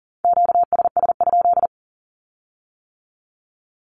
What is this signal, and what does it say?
Morse code, heard as a keyed tone, about 38 wpm, 700 Hz. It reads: QSS?
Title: Q-Codes with Spoken Meaning - 40wpm